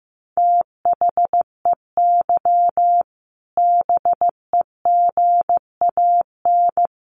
Morse code, heard as a keyed tone, about 15 wpm, 700 Hz. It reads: THEY BEGAN
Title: A Sound Of Thunder 15wpm